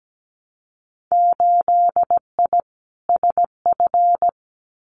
Morse code, heard as a keyed tone, about 17 wpm, 700 Hz. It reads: 8I SF
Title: Winter Field Day Exchanges Encore 17wpm